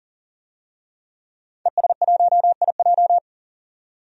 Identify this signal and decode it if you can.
Morse code, heard as a keyed tone, about 40 wpm, 700 Hz. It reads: ES1IJ
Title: DXCC-100 Call Signs - 5 Characters - 40wpm